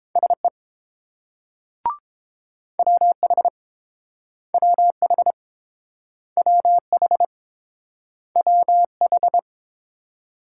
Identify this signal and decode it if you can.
Morse code, heard as a keyed tone, about 30 wpm, 700 Hz. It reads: SE WH WH WH WH